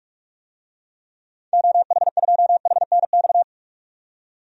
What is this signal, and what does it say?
Morse code, heard as a keyed tone, about 46 wpm, 700 Hz. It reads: OH2HNX
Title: DXCC-100 Call Signs - 6 Characters - 45wpm